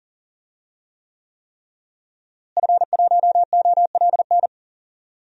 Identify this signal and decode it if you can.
Morse code, heard as a keyed tone, about 40 wpm, 700 Hz. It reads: F1OLN
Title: DXCC-100 Call Signs - 5 Characters - Repeated 3x - 40wpm